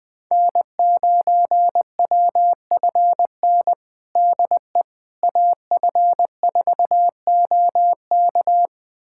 Morse code, heard as a keyed tone, about 20 wpm, 700 Hz. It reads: N9WFN DE AF4OK
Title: USA Callsign DE Callsign - 5 Characters - 20wpm